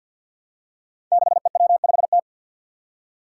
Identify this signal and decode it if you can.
Morse code, heard as a keyed tone, about 52 wpm, 700 Hz. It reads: 6EW5T
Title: Five Challenging Letter-Number Combinations 50wpm